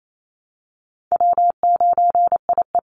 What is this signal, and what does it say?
Morse code, heard as a keyed tone, about 28 wpm, 700 Hz. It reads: W9IE
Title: Call Signs Repeated 2x - 28wpm